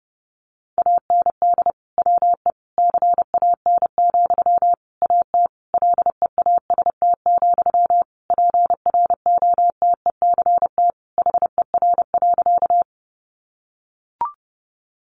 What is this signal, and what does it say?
Morse code, heard as a keyed tone, about 30 wpm, 700 Hz. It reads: AND WE CAN, AT LEAST, PROTECT HER.
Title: Sentences from Top 900 Words 30wpm